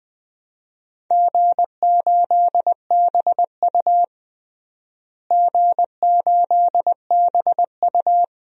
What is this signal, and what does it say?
Morse code, heard as a keyed tone, about 20 wpm, 700 Hz. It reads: G8BU G8BU
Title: DXCC-100 Call Signs - 4 Characters - Repeated 3x - 20wpm